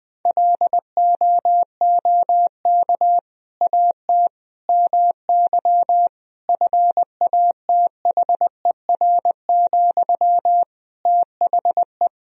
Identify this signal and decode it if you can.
Morse code, heard as a keyed tone, about 20 wpm, 700 Hz. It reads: LOOK AT MY FATHER, THE
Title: Sentences from Top 1000 Words - Review Part 3 - 20wpm